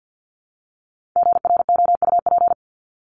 Morse code, heard as a keyed tone, about 50 wpm, 700 Hz. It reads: GROUP
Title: Top 300 Words 50wpm